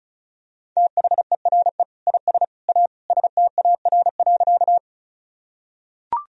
Morse code, heard as a keyed tone, about 36 wpm, 700 Hz. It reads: THERE IS A STAR.
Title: Sentences from Top 1000 Words - Review Part 2 - 35wpm